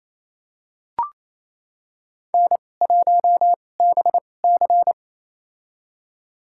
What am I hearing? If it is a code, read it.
Morse code, heard as a keyed tone, about 28 wpm, spaced as for 20 wpm, 700 Hz. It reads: N1BC
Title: USA Call Signs - 4 Characters - 2x Character Spacing - 28wpm